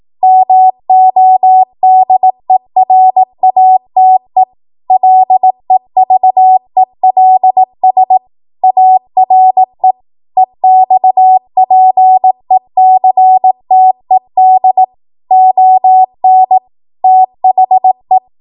Morse code, heard as a keyed tone, about 18 wpm, 750 Hz. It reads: MODERATE LEVELS ARE EXPECTED ON THE
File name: Morse / 18WPM